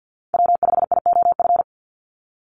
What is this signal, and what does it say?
Morse code, heard as a keyed tone, about 50 wpm, 700 Hz. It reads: W5IOF
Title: USA Call Signs 50wpm